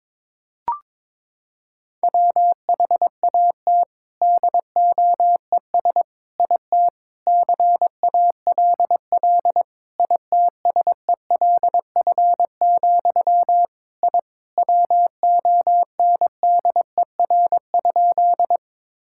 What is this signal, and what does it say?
Morse code, heard as a keyed tone, about 22 wpm, 700 Hz. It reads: WHAT DOES IT CALL ITSELF, I WONDER?
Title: Sentences from Top 1000 Words - Review Part 5 - 22wpm